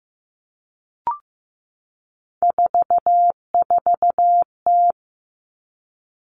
Morse code, heard as a keyed tone, about 15 wpm, 700 Hz. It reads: 44T
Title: Three Challenging Letter-Number Combinations 15wpm